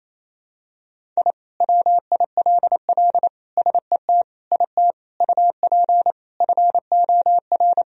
Morse code, heard as a keyed tone, about 28 wpm, 700 Hz. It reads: I WILL SET IT UP FOR
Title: Sentences from Top 1000 Words - Review Part 1 - 28wpm